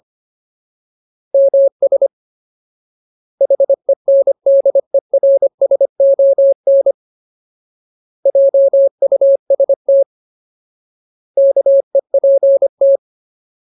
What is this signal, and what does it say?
Morse code, heard as a keyed tone, about 25 wpm, 550 Hz. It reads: MS HENDERSON JUST KEPT
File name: Ultimate Word List / 25WPM Ultimate